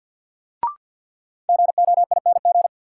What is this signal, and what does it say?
Morse code, heard as a keyed tone, about 53 wpm, 700 Hz. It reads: GOING
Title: Sets of 2 Words - New Word Drill - 50wpm